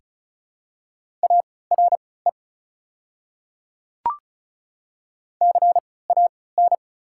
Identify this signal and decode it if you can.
Morse code, heard as a keyed tone, about 34 wpm, spaced as for 20 wpm, 690 Hz. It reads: ARE CAN
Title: 3x Character Spacing - Top 500 Words - 35wpm